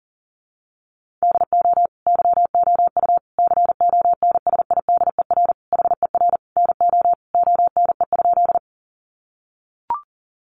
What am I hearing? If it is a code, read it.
Morse code, heard as a keyed tone, about 40 wpm, 700 Hz. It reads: DO YOU CONSIDER HER NO ONE?
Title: Sentences from Top 600 Words 40wpm